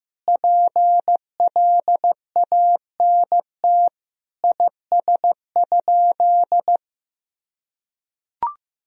Morse code, heard as a keyed tone, about 15 wpm, 700 Hz. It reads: PLANT IS?
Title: Sentences from Top 1000 Words - Review Part 1 - 15wpm